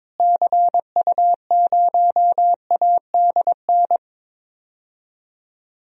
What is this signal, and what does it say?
Morse code, heard as a keyed tone, about 21 wpm, 700 Hz. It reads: CU0ADN
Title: DXCC-100 Call Signs - 6 Characters - Repeated 3x - 22wpm